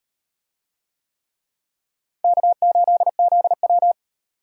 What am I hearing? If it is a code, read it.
Morse code, heard as a keyed tone, about 38 wpm, 700 Hz. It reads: K8ZW